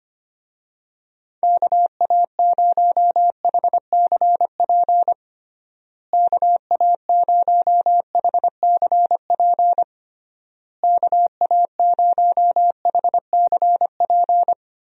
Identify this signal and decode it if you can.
Morse code, heard as a keyed tone, about 25 wpm, 700 Hz. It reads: KA0HCP KA0HCP KA0HCP
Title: USA Call Signs - 6 Characters - Repeated 3x - 25wpm